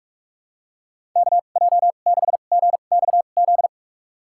Morse code, heard as a keyed tone, about 46 wpm, spaced as for 32 wpm, 700 Hz. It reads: KJ6GXZ
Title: USA Call Signs - 6 Characters - 2x Character Spacing - 45wpm